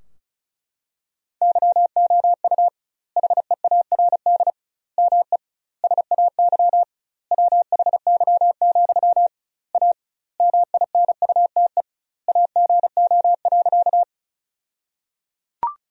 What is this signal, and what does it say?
Morse code, heard as a keyed tone, about 36 wpm, 700 Hz. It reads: YOU HEARD ME SAY WHY, A MINUTE AGO.
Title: 2x Word-Spacing - Sentences from Top 500 Words - 35wpm